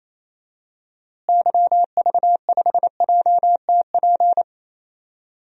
Morse code, heard as a keyed tone, about 28 wpm, 700 Hz. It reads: YV5JTP
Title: DXCC-100 Call Signs - 6 Characters - Repeated 3x - 28wpm